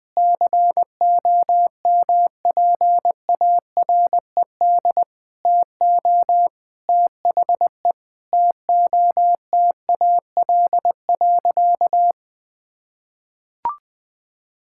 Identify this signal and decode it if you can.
Morse code, heard as a keyed tone, about 20 wpm, 700 Hz. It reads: COMPARED TO THE TOTAL.
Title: Sentences with 8-Word Phrases - Spoken Beginning - 20wpm